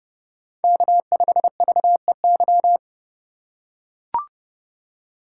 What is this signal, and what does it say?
Morse code, heard as a keyed tone, about 30 wpm, 700 Hz. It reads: K5VEY
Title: USA Call Signs - 5 Characters - 4x Character Spacing - 30wpm